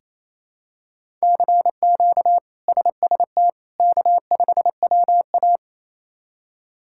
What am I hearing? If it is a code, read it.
Morse code, heard as a keyed tone, about 28 wpm, 700 Hz. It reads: CQ SST K5WA
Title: K1USN SST QSO Practice - 28wpm